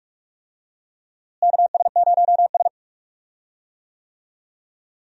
Morse code, heard as a keyed tone, about 46 wpm, 700 Hz. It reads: KS0S
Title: USA Call Signs - 4 Characters - Repeated 2x - 45wpm